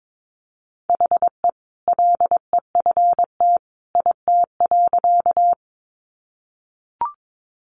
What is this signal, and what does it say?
Morse code, heard as a keyed tone, about 22 wpm, 700 Hz. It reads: HE LEFT IT.
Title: A Sound Of Thunder 22wpm